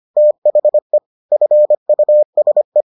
Morse code, heard as a keyed tone, about 25 wpm, 600 Hz. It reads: THE FUSE